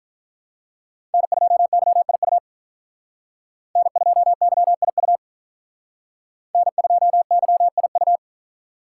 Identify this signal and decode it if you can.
Morse code, heard as a keyed tone, about 44 wpm, 700 Hz. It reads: N2YIU N2YIU N2YIU